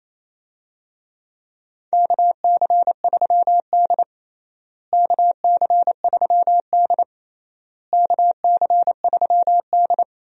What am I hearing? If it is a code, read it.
Morse code, heard as a keyed tone, about 27 wpm, 700 Hz. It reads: KC3D KC3D KC3D
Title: USA Call Signs - 4 Characters - Repeated 3x - 28wpm